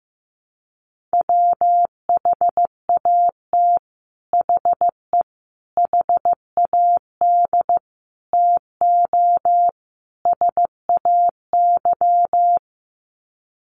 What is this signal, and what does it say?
Morse code, heard as a keyed tone, about 15 wpm, 700 Hz. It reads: WHAT HE HAD TO SAY
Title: Sets of 5 Words Encore2 15wpm